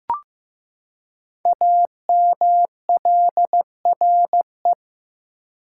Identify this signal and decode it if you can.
Morse code, heard as a keyed tone, about 15 wpm, 690 Hz. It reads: AMLRE
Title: Code Groups - 5 Letters - 15wpm